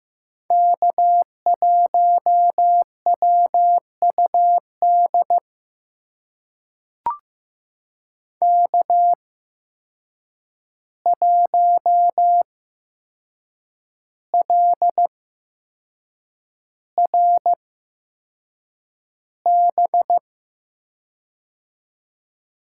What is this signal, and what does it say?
Morse code, heard as a keyed tone, about 15 wpm, spaced as for 6 wpm, 700 Hz. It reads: K1WUD K1LRB
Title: USA Call Signs - 5 Characters - 8x Character Spacing - 15wpm